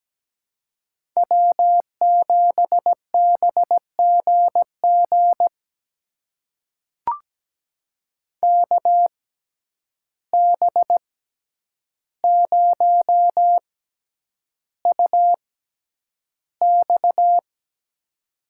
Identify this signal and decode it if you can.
Morse code, heard as a keyed tone, about 17 wpm, spaced as for 9 wpm, 700 Hz. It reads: W7BGG KB0UX
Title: USA Call Signs - 5 Characters - 6x Character Spacing - 17wpm